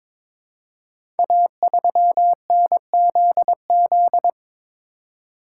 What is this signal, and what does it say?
Morse code, heard as a keyed tone, about 23 wpm, 700 Hz. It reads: A3NZZ
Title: Code Groups - 5 Letter:Numbers - 22wpm